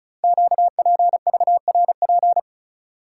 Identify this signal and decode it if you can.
Morse code, heard as a keyed tone, about 36 wpm, 700 Hz. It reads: QPVRP
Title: Code Groups - 5 Letters - 35wpm